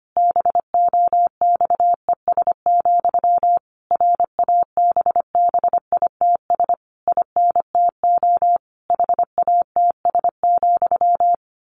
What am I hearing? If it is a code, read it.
Morse code, heard as a keyed tone, about 25 wpm, 700 Hz. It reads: BOXES, RABBITS INTO HATS,
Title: A Sound Of Thunder 25wpm